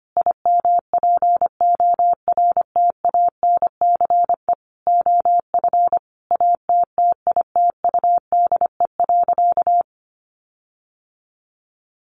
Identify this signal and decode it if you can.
Morse code, heard as a keyed tone, about 25 wpm, 700 Hz. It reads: IMPORTANCE OF ATTITUDE.
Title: Sentences from Top 1300 Words 25wpm